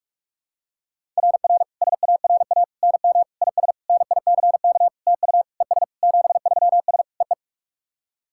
Morse code, heard as a keyed tone, about 43 wpm, 700 Hz. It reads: RR SARA NM IS NICK TU ES 73S EE
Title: SOTA QSO Practice 45wpm